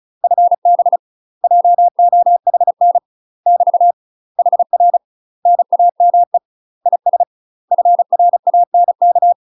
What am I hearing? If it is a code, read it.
Morse code, heard as a keyed tone, about 34 wpm, 700 Hz. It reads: FB JOHN <BT> HR NAME IS FRANK
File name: UK MASTER QSOs TRAINING / 35WPM / UK MASTER QSOs 0030